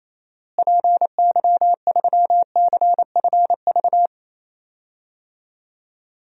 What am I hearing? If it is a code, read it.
Morse code, heard as a keyed tone, about 28 wpm, 700 Hz. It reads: PY3CFV
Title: DXCC-100 Call Signs - 6 Characters - Repeated 2x - 28wpm